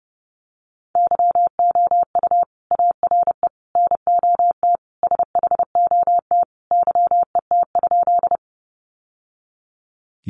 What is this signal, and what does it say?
Morse code, heard as a keyed tone, about 30 wpm, 700 Hz. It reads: YOU ARE NOT SHOT YET?
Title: Sentences from Top 1100 Words 30wpm